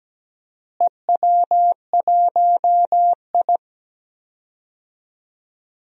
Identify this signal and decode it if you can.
Morse code, heard as a keyed tone, about 17 wpm, 700 Hz. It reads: EW1I
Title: DXCC-100 Call Signs - 4 Characters - Repeated 2x - 17wpm